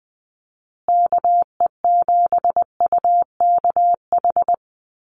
Morse code, heard as a keyed tone, about 20 wpm, 700 Hz. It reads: KE7UKH
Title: USA Call Signs Repeated 3x 20wpm